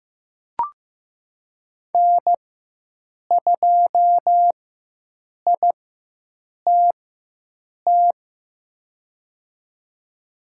Morse code, heard as a keyed tone, about 15 wpm, spaced as for 7 wpm, 700 Hz. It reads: N2ITT
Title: USA Call Signs - 5 Characters - 4x Character Spacing - 15wpm